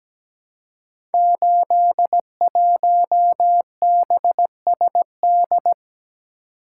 Morse code, heard as a keyed tone, about 17 wpm, 700 Hz. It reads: 81BSD
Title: Code Groups - 5 Letter:Numbers - 17wpm